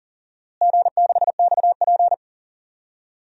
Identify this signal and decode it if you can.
Morse code, heard as a keyed tone, about 40 wpm, 700 Hz. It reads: G6XP
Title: DXCC-100 Call Signs - 4 Characters - Repeated 3x - 40wpm